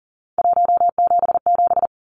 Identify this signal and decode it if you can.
Morse code, heard as a keyed tone, about 39 wpm, 700 Hz. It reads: 177
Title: Numbers - 3-Digit 40wpm